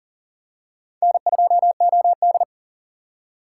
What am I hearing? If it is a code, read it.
Morse code, heard as a keyed tone, about 38 wpm, 700 Hz. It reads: N2OD
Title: USA Call Signs - 6x Character Spacing - 40wpm